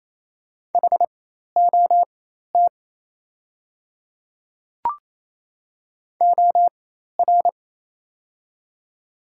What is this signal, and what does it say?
Morse code, heard as a keyed tone, about 28 wpm, spaced as for 13 wpm, 700 Hz. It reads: HOT OR
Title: 4x Character Spacing - Top 100 Words - 28wpm